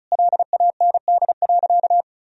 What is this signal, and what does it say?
Morse code, heard as a keyed tone, about 36 wpm, 700 Hz. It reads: LAND.
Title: Sentences from Top 1000 Words - Review Part 4 - 35wpm